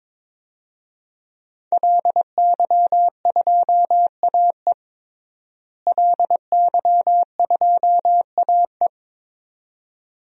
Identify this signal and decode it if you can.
Morse code, heard as a keyed tone, about 22 wpm, 700 Hz. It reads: LY2AE LY2AE
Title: DXCC-100 Call Signs - 5 Characters - Repeated 2x - 22wpm